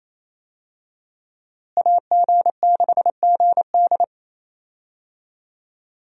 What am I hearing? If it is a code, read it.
Morse code, heard as a keyed tone, about 28 wpm, 700 Hz. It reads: AG6GD
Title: USA Call Signs - 5 Characters - Repeated 2x - 28wpm